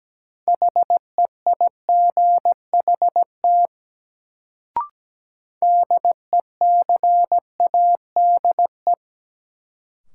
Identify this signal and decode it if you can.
Morse code, heard as a keyed tone, about 17 wpm, 700 Hz. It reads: HEIGHT DECADE
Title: Sets of 5 Words - New Word Drill - 17wpm